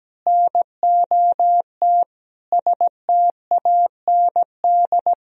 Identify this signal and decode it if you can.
Morse code, heard as a keyed tone, about 17 wpm, 700 Hz. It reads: NOT STAND
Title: Sentences from Top 1000 Words - Review Part 1 - 17wpm